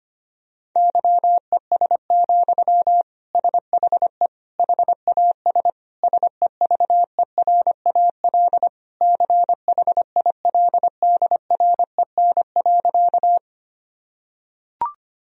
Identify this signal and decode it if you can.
Morse code, heard as a keyed tone, about 25 wpm, 700 Hz. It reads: YES, SHE HAS SEVERAL CHILDREN.
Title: Sentences from Top 1000 Words - Review Part 2 - 25wpm